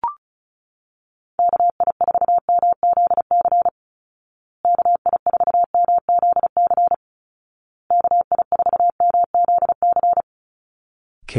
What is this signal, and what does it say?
Morse code, heard as a keyed tone, about 34 wpm, 700 Hz. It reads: KI4MZC KI4MZC KI4MZC
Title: Call Signs Repeated 2x - 35wpm